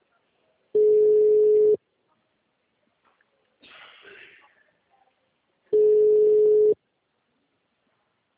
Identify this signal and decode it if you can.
Ringback tone (Europe, 425 Hz), with speech.